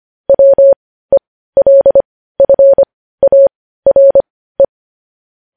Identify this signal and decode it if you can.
Morse code, heard as a keyed tone, about 25 wpm, spaced as for 15 wpm, 550 Hz. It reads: WELFARE
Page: news headlines in morse code at 15 wpm